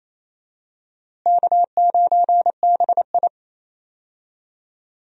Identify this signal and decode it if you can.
Morse code, heard as a keyed tone, about 28 wpm, 700 Hz. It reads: K9BI
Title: USA Call Signs - 4 Characters - Repeated 2x - 28wpm